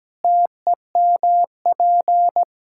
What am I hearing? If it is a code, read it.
Morse code, heard as a keyed tone, about 17 wpm, 700 Hz. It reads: TEMP
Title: Practice QSO - Exchange 2 - Info - 17wpm